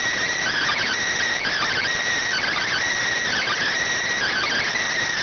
Emergency Alert System (SAME) tones (header bursts), heard as harsh buzzing bursts. It is heard from 0.4 to 4.9 s.